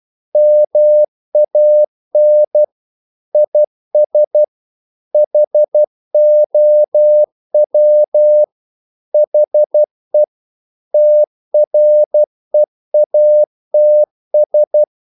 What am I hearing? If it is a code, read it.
Morse code, heard as a keyed tone, about 12 wpm, 600 Hz. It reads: MAN IS HOW HE TREATS
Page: English quotations in CW (telegraphy - morse)